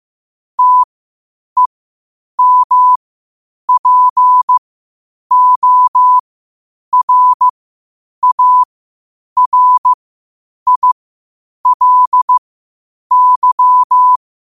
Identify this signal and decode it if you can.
Morse code, heard as a keyed tone, about 15 wpm, spaced as for 8 wpm, 1000 Hz. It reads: TEMPORARILY
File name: qotdfiles / 8wpm